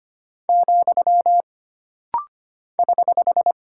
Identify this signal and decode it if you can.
Morse code, heard as a keyed tone, about 25 wpm, 700 Hz. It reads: , <HH>
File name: Single Punctuation-Prosign - Rapid-Fire - 25wpm